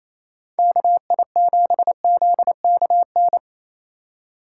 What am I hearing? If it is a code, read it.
Morse code, heard as a keyed tone, about 34 wpm, 700 Hz. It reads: KI7ZKN
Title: Speed Racing - USA Call Signs 28wpm